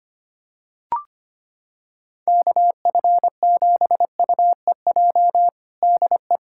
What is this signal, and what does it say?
Morse code, heard as a keyed tone, about 25 wpm, 700 Hz. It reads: KF7UEJ DE
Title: Practice QSO - Exchange 4 - Ending - 25wpm